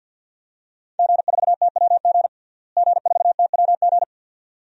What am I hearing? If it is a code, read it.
Morse code, heard as a keyed tone, about 47 wpm, 700 Hz. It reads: G4TWG G4TWG
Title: DXCC-100 Call Signs - 5 Characters - Repeated 2x - 50wpm